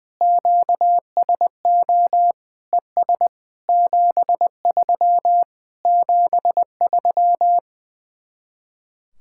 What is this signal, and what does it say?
Morse code, heard as a keyed tone, about 20 wpm, 700 Hz. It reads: QSO ES 73 73
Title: Speed-Racing Common QSO Phrases - 20wpm